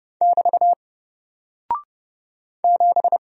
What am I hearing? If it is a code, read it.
Morse code, heard as a keyed tone, about 30 wpm, 700 Hz. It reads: <BT> 7
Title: Single Character (Letter:Number:Punctuation:Prosign) Alt 30wpm